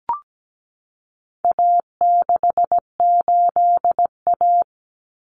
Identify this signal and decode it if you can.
Morse code, heard as a keyed tone, about 17 wpm, 700 Hz. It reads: A68A
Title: Common Cut Numbers - 4-Digit 17wpm